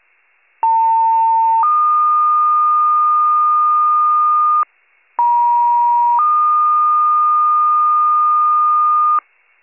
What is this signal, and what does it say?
Two-tone page, heard as a one-second tone, then a three-second tone, about 900 then 1220 Hz, about 950 then 1220 Hz. It is heard from 0.6 to 9.2 s.